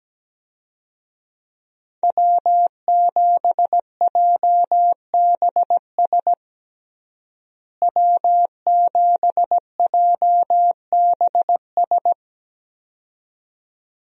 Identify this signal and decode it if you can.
Morse code, heard as a keyed tone, about 17 wpm, 700 Hz. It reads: W7JBS W7JBS
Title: USA Call Signs - 5 Characters - Repeated 2x - 17wpm